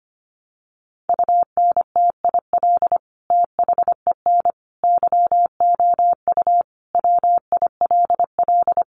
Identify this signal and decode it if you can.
Morse code, heard as a keyed tone, about 25 wpm, 700 Hz. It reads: UNTIL THEN YOU WILL
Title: Sentences from Top 600 Words 25wpm